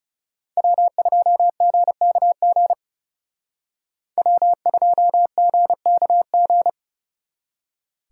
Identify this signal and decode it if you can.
Morse code, heard as a keyed tone, about 39 wpm, 700 Hz. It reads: W2GKG W2GKG